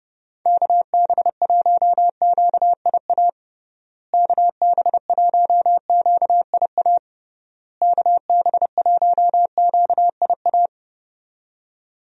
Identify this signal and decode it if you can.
Morse code, heard as a keyed tone, about 32 wpm, 700 Hz. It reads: KB1QIA KB1QIA KB1QIA
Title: USA Call Signs - 6 Characters - Repeated 3x - 30wpm